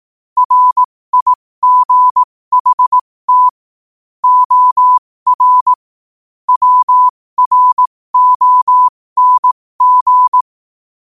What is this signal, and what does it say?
Morse code, heard as a keyed tone, about 18 wpm, spaced as for 15 wpm, 1000 Hz. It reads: RIGHT OR WRONG
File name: qotdfiles / 15wpm